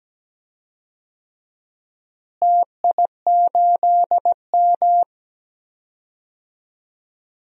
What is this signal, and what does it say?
Morse code, heard as a keyed tone, about 17 wpm, 700 Hz. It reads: TI8M
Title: DXCC-100 Call Signs - 4 Characters - Repeated 2x - 17wpm